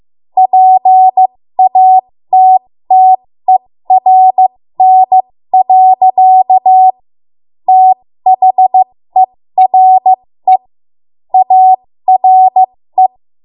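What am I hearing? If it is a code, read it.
Morse code, heard as a keyed tone, about 15 wpm, spaced as for 13 wpm, 750 Hz. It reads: PATTERN. THERE ARE
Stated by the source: ARRL Morse code practice at 13 WPM.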